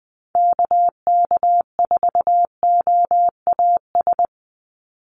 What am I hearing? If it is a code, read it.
Morse code, heard as a keyed tone, about 20 wpm, 700 Hz. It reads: KK4OAS
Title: USA Call Signs 20wpm